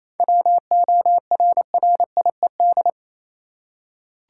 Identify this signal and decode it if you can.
Morse code, heard as a keyed tone, about 28 wpm, 700 Hz. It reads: WORRIED
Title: Seven Letter Words - 28wpm